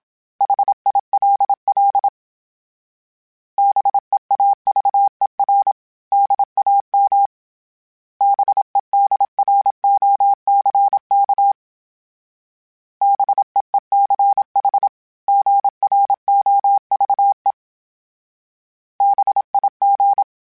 Morse code, heard as a keyed tone, about 27 wpm, 800 Hz. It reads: HILL BEAVER DAM BEDROCK BEECH GROVE BIG
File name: fromSMs / FromSMs mp3s / Cities@25